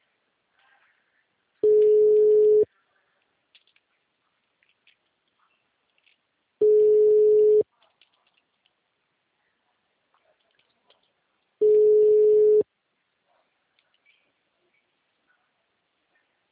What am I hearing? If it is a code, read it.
Ringback tone (Europe, 425 Hz).